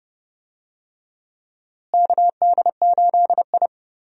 Morse code, heard as a keyed tone, about 30 wpm, 700 Hz. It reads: KD8I
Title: USA Call Signs - 4 Characters - Repeated 2x - 30wpm